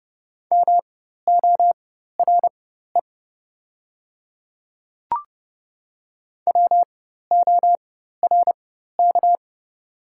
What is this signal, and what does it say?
Morse code, heard as a keyed tone, about 30 wpm, spaced as for 14 wpm, 700 Hz. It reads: MORE WORK
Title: 4x Character Spacing - Top 200 Words - 30wpm